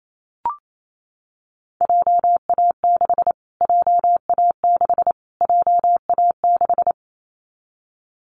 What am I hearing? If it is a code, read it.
Morse code, heard as a keyed tone, about 28 wpm, 700 Hz. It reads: JA6 JA6 JA6
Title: Callsign Prefixes for Top 25 DXCC - Repeated 3x - 28wpm